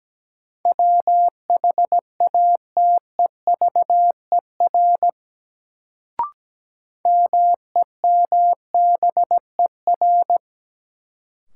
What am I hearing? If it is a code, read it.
Morse code, heard as a keyed tone, about 17 wpm, 700 Hz. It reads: WHATEVER MEMBER
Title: Sentences from Top 1100 Words - New Word Drill Alt - 17wpm